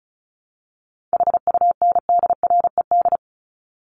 Morse code, heard as a keyed tone, about 34 wpm, 700 Hz. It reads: HUNDRED
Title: Top 500 Words Repeated 35wpm